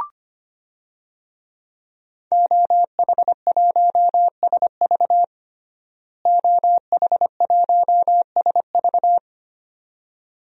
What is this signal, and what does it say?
Morse code, heard as a keyed tone, about 25 wpm, 700 Hz. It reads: OH1SV OH1SV
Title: DXCC-100 Call Signs - 5 Characters - Repeated 2x - 25wpm